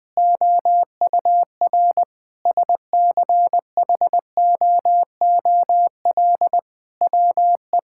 Morse code, heard as a keyed tone, about 20 wpm, 700 Hz. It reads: OUR SCHOOL WE
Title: Sentences with 8-Word Phrases - Spoken Beginning - 20wpm